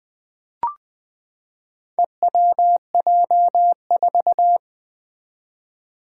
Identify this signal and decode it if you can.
Morse code, heard as a keyed tone, about 21 wpm, 700 Hz. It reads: EWJ4
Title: Four Challenging Letter-Number Combinations 20wpm